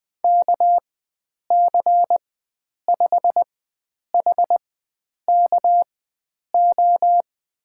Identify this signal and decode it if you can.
Morse code, heard as a keyed tone, about 20 wpm, spaced as for 9 wpm, 700 Hz. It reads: KC5HKO
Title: USA Call Signs - 4x Character Spacing - 20wpm